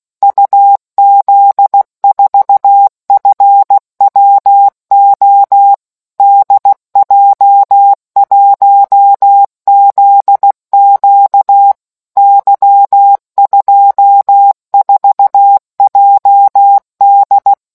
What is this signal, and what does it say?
Morse code, heard as a keyed tone, about 16 wpm, 800 Hz. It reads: UZ4FWO DJ1ZQ Y24JD